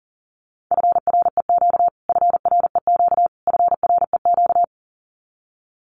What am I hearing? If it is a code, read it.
Morse code, heard as a keyed tone, about 37 wpm, 700 Hz. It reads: FREQ FREQ FREQ
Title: Abbreviations Repeated 3x - 40wpm